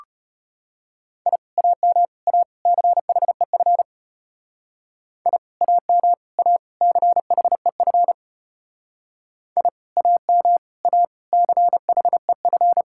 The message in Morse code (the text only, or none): I AM A CHEF I AM A CHEF I AM A CHEF